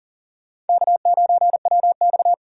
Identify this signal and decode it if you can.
Morse code, heard as a keyed tone, about 40 wpm, 700 Hz. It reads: K9WX
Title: USA Call Signs - 4 Characters - 40wpm